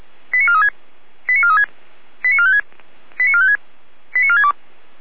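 Five-tone selective call (ZVEI), heard as quick runs of five tones, tones of about 70 ms. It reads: 89436 89436 89456 89456 89462
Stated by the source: ZVEI five-tone selective call received by SDR.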